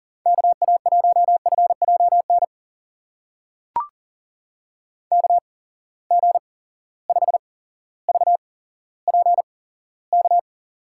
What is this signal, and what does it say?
Morse code, heard as a keyed tone, about 40 wpm, spaced as for 17 wpm, 700 Hz. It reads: KA1FJN KG5VPK
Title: USA Call Signs - 6 Characters - 8x Character Spacing - 40wpm